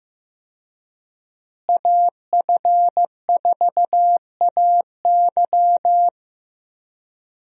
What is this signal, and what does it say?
Morse code, heard as a keyed tone, about 15 wpm, 700 Hz. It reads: AF4AY
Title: USA Call Signs - 5 Characters - Repeated 3x - 15wpm